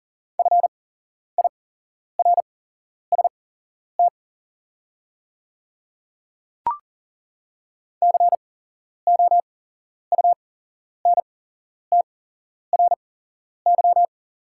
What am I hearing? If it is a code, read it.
Morse code, heard as a keyed tone, about 40 wpm, spaced as for 11 wpm, 700 Hz. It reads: FIRST COUNTRY
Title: 8x Character Spacing - Top 500 Words - 40wpm